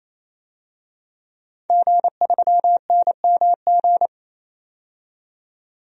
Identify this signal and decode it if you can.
Morse code, heard as a keyed tone, about 28 wpm, 700 Hz. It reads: G3NMG
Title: DXCC-100 Call Signs - 5 Characters - 28wpm